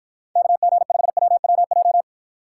Morse code, heard as a keyed tone, about 53 wpm, 700 Hz. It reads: KG5WWJ